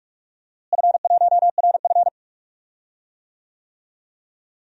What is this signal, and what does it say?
Morse code, heard as a keyed tone, about 41 wpm, 700 Hz. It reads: F1RF
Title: DXCC-100 Call Signs - 4 Characters - Repeated 2x - 45wpm